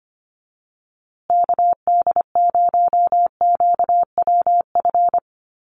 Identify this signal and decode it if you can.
Morse code, heard as a keyed tone, about 25 wpm, 700 Hz. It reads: KD0QWF
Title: USA Call Signs Repeated 3x 25wpm